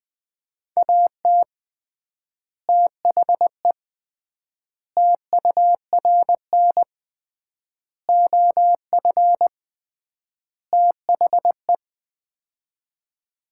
Morse code, heard as a keyed tone, about 20 wpm, 700 Hz. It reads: AT THE TURN OF THE
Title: 3x Word-Spacing - Sets of 5 Words - 20wpm